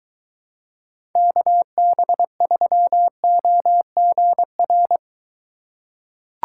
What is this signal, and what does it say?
Morse code, heard as a keyed tone, about 23 wpm, 700 Hz. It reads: KB3OGR